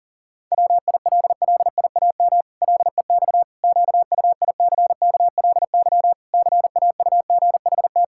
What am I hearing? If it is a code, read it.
Morse code, heard as a keyed tone, about 40 wpm, 700 Hz. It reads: WILLIAM LEX QUICKLY CAUGHT
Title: Pangram Challenge v2 - 40wpm